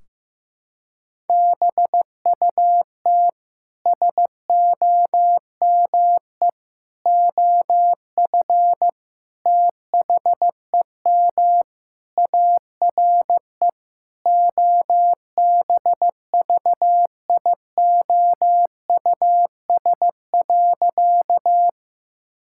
Morse code, heard as a keyed tone, about 16 wpm, 700 Hz. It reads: BUT SOME OF THEM ARE OBVIOUS.
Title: Sentences from Top 1600 Words 15wpm